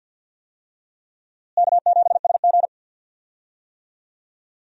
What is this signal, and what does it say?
Morse code, heard as a keyed tone, about 53 wpm, 700 Hz. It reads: K7SG
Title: USA Call Signs - 4 Characters - Repeated 3x - 50wpm